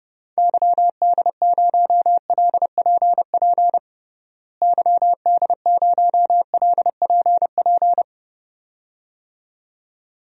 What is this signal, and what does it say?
Morse code, heard as a keyed tone, about 30 wpm, 700 Hz. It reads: YD0LPP YD0LPP
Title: DXCC-100 Call Signs - 6 Characters - Repeated 2x - 30wpm